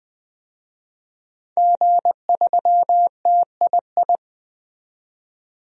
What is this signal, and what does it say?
Morse code, heard as a keyed tone, about 20 wpm, 700 Hz. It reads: G3TII
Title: DXCC-100 Call Signs - 5 Characters - 20wpm